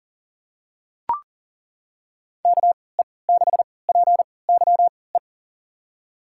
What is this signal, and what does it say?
Morse code, heard as a keyed tone, about 40 wpm, spaced as for 23 wpm, 700 Hz. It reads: KE6PYE
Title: USA Call Signs - 6 Characters - 3x Character Spacing - 40wpm